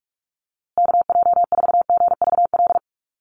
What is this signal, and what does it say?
Morse code, heard as a keyed tone, about 44 wpm, 690 Hz. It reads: KJ4GVL
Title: USA Call Signs Repeated 3x 45wpm